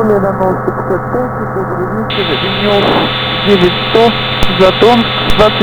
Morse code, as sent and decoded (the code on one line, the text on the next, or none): none